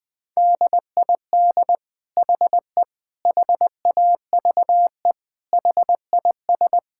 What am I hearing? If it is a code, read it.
Morse code, heard as a keyed tone, about 19 wpm, 700 Hz. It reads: DID HE HAVE HIS
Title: Sentences from Top 1000 Words - Review Part 5 - 20wpm